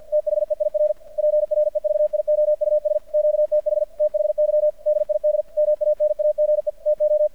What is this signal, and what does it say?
Morse code, heard as a keyed tone, about 50 wpm, 600 Hz. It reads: THEIR PREVIOUS QTH THY DID MANAGE TO
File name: archived / CRSnet